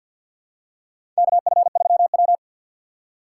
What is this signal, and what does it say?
Morse code, heard as a keyed tone, about 51 wpm, 700 Hz. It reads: KF3W
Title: USA Call Signs - 4 Characters - 2x Character Spacing - 50wpm